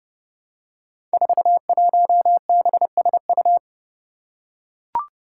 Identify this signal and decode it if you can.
Morse code, heard as a keyed tone, about 30 wpm, 700 Hz. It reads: 41BSU
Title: Five Challenging Letter-Number Combinations 30wpm